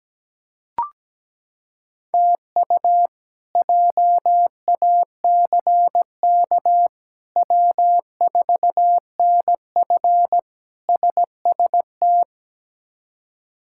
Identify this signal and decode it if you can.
Morse code, heard as a keyed tone, about 17 wpm, 700 Hz. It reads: TU JACK W4NF SST
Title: K1USN SST QSO Practice - 17wpm